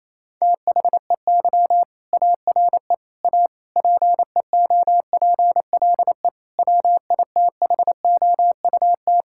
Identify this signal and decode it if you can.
Morse code, heard as a keyed tone, about 27 wpm, 700 Hz. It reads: THEY ARE A PEOPLE WITHOUT
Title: Sentences from Top 1000 Words - Review Part 6 - 28wpm